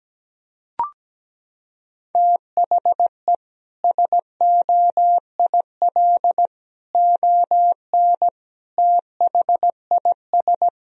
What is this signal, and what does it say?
Morse code, heard as a keyed tone, about 17 wpm, 700 Hz. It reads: THE SOIL ON THIS
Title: Sentences from Top 1000 Words - Review Part 4 - 17wpm